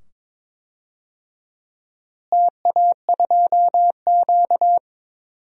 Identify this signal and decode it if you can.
Morse code, heard as a keyed tone, about 22 wpm, 700 Hz. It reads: TA2Q
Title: DXCC-100 Call Signs - 4 Characters - Repeated 2x - 22wpm